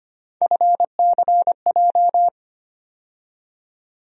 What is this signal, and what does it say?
Morse code, heard as a keyed tone, about 25 wpm, 700 Hz. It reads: FCJ
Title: USA Call Sign Postfixes - 3 Letters 25wpm